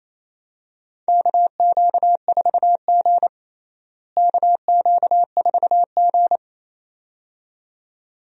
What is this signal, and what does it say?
Morse code, heard as a keyed tone, about 28 wpm, 700 Hz. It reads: KQ4G KQ4G
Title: USA Call Signs - 4 Characters - Repeated 2x - 28wpm